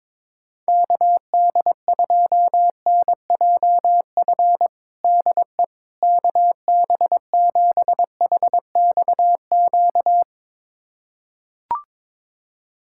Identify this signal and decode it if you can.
Morse code, heard as a keyed tone, about 22 wpm, 700 Hz. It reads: KD2NJF DE KB7HXQ
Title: USA Callsign DE Callsign - 6 Characters - 22wpm